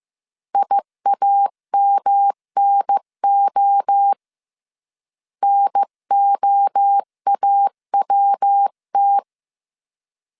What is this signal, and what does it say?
Morse code, heard as a keyed tone, about 15 wpm, 800 Hz. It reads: IAMNO NOAWT
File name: CW / DET / 1 3 1500C S A 14wpm 2447